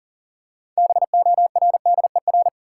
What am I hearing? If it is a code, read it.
Morse code, heard as a keyed tone, about 37 wpm, 700 Hz. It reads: BORDER
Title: Six Letter Words - 40wpm